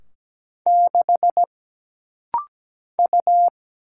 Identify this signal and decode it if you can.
Morse code, heard as a keyed tone, about 17 wpm, 700 Hz. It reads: 6 U
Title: Single Letter-Number Alt - Rapid-Fire 17wpm